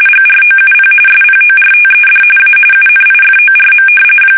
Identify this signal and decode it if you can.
RTTY, about 45.45 baud. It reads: GROUP SPORTED A UNIFORM
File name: fsk850 / battleships